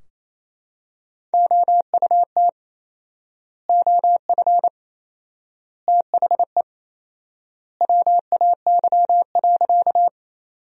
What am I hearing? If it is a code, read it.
Morse code, heard as a keyed tone, about 28 wpm, 700 Hz. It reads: OUT OF THE WAY.
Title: 4x Word-Spacing - Sentences from Top 100 Words - 28wpm